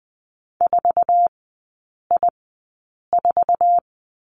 Morse code, heard as a keyed tone, about 20 wpm, spaced as for 8 wpm, 700 Hz. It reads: 4I4
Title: 047 Lesson 9e - Number 4 - Three Characters 20wpm